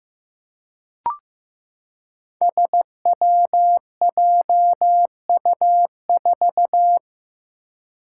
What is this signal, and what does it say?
Morse code, heard as a keyed tone, about 15 wpm, 700 Hz. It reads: SWJU4
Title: Five Challenging Letter-Number Combinations 15wpm